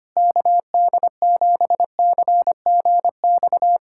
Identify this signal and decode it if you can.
Morse code, heard as a keyed tone, about 25 wpm, 700 Hz. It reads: KD7CGX
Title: USA Callsign DE Callsign - 6 Characters - 25wpm